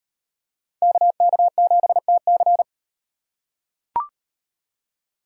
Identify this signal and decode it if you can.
Morse code, heard as a keyed tone, about 38 wpm, 700 Hz. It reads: KK7TC